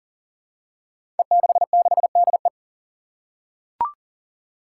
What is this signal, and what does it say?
Morse code, heard as a keyed tone, about 37 wpm, 700 Hz. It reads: E66DE
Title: Five Challenging Letter-Number Combinations 40wpm